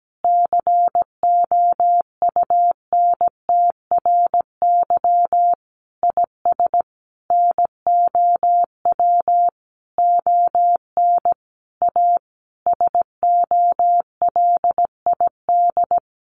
Morse code, heard as a keyed tone, about 17 wpm, 700 Hz. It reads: COUNTRY IS NOW ON A SOLID
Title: Sentences from Top 1500 Words 17wpm